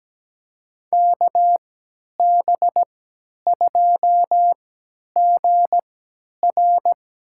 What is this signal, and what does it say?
Morse code, heard as a keyed tone, about 17 wpm, spaced as for 10 wpm, 700 Hz. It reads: KB2GR
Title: USA Call Signs - 3x Character Spacing - 17wpm